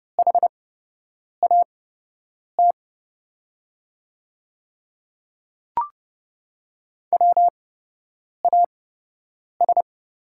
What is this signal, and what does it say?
Morse code, heard as a keyed tone, about 30 wpm, spaced as for 8 wpm, 700 Hz. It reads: HAT WAS
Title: 8x Character Spacing - Top 500 Words - 30wpm